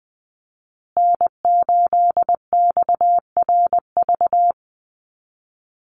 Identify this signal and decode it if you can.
Morse code, heard as a keyed tone, about 20 wpm, 700 Hz. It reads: N8XRV
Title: USA Call Signs Repeated 3x 20wpm